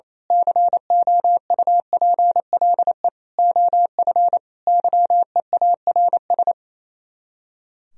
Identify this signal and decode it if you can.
Morse code, heard as a keyed tone, about 27 wpm, 700 Hz. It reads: COUPLE OF YEARS
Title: Speed-Racing - Sets of 5 Words 28wpm